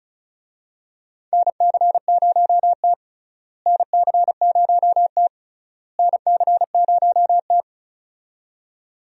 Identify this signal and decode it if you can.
Morse code, heard as a keyed tone, about 36 wpm, 700 Hz. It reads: NC0T NC0T NC0T
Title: USA Call Signs - 4 Characters - Repeated 3x - 35wpm